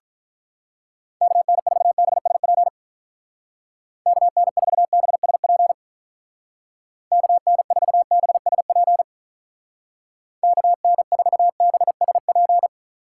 Morse code, heard as a keyed tone, about 44 wpm, 700 Hz. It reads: KN4BSP KN4BSP KN4BSP KN4BSP
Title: Speed Racing - USA 2x3 Call Signs 35wpm